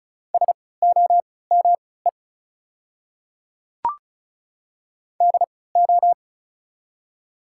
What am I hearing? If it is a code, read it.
Morse code, heard as a keyed tone, about 36 wpm, spaced as for 20 wpm, 700 Hz. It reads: SOME DO
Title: 3x Character Spacing - Top 100 Words - 35wpm